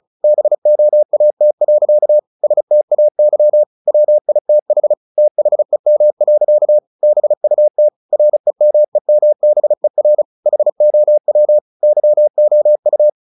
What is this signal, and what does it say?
Morse code, heard as a keyed tone, about 34 wpm, 600 Hz. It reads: BOAT. STAY WITH THEM. BUT REMEMBER HOW YOU
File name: the old man and the sea / the old man and the sea 1 35wpm